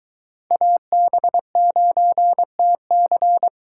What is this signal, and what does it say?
Morse code, heard as a keyed tone, about 22 wpm, 700 Hz. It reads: AB9TC